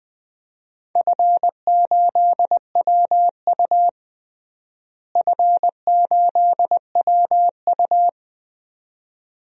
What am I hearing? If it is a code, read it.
Morse code, heard as a keyed tone, about 20 wpm, 700 Hz. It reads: F8WU F8WU
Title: DXCC-100 Call Signs - 4 Characters - Repeated 2x - 20wpm